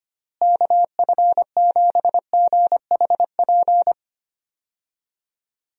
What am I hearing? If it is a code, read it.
Morse code, heard as a keyed tone, about 25 wpm, 700 Hz. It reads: KF7GHP
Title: USA Call Signs - 6 Characters - Repeated 2x - 25wpm